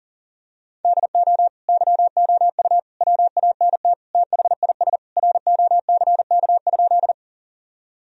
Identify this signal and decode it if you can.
Morse code, heard as a keyed tone, about 40 wpm, 700 Hz. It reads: DO YOU WANT THIS ROCK?
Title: Sentences from Top 1000 Words - Review Part 2 - 40wpm